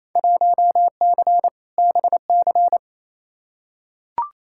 Morse code, heard as a keyed tone, about 28 wpm, 700 Hz. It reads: 1C BC
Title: Field Day Exchanges Encore - 28wpm